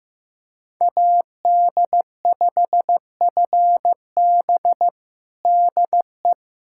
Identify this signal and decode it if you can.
Morse code, heard as a keyed tone, about 15 wpm, 700 Hz. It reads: AD5FB DE
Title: USA Callsign DE Callsign - 5 Characters - 15wpm